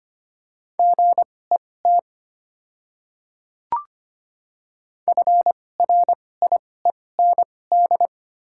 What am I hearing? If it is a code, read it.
Morse code, heard as a keyed tone, about 25 wpm, spaced as for 18 wpm, 700 Hz. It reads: GET FRIEND
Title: 2x Character Spacing - Top 300 Words - 25wpm